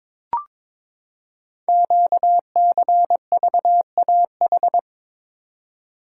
Morse code, heard as a keyed tone, about 22 wpm, 700 Hz. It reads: QCVAH
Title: Code Groups - 5 Letters - 22wpm